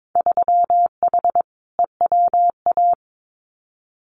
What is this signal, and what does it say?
Morse code, heard as a keyed tone, about 22 wpm, 700 Hz. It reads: 3H EWA
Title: Winter Field Day Exchanges Encore 22wpm